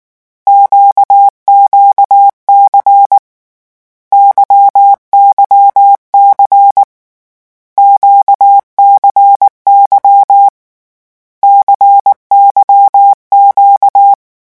Morse code, heard as a keyed tone, about 19 wpm, 800 Hz. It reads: QQC YYC QCY CYQ
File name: morse mp3 / lecon2-vitesse 18